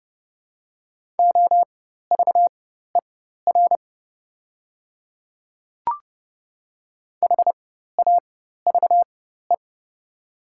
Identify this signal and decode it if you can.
Morse code, heard as a keyed tone, about 30 wpm, spaced as for 14 wpm, 700 Hz. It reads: OVER HAVE
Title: 4x Character Spacing - Top 100 Words - 30wpm